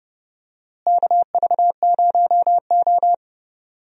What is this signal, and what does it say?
Morse code, heard as a keyed tone, about 30 wpm, 700 Hz. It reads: KV0O
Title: USA Call Signs - 4 Characters - Repeated 2x - 30wpm